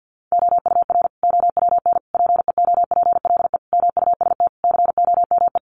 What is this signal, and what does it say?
Morse code, heard as a keyed tone, about 50 wpm, 700 Hz. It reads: OUR OWN PEOPLE MUST COME
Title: Sentences from Top 300 Words 50wpm